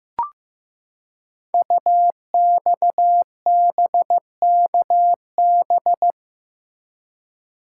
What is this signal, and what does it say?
Morse code, heard as a keyed tone, about 15 wpm, 700 Hz. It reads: UXBKB
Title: Code Groups - 5 Letters - 15wpm